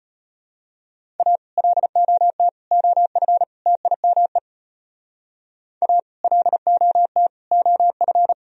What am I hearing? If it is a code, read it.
Morse code, heard as a keyed tone, about 32 wpm, 700 Hz. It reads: A LOT OF TIME A LOT OF